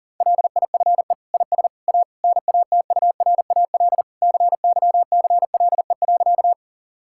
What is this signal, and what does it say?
Morse code, heard as a keyed tone, about 37 wpm, 700 Hz. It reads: LIFE IS A NATURAL CYCLE.
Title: Sentences from Top 1000 Words - Review Part 6 - 40wpm